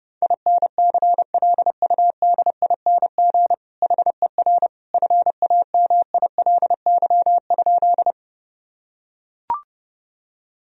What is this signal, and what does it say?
Morse code, heard as a keyed tone, about 30 wpm, 700 Hz. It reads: INCLUDING HER FAMILY?
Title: Sentences from Top 1000 Words - Review Part 4 - 30wpm